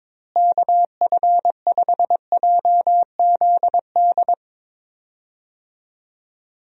Morse code, heard as a keyed tone, about 22 wpm, 700 Hz. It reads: KF5JZD
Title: USA Call Signs - 6 Characters - Repeated 2x - 22wpm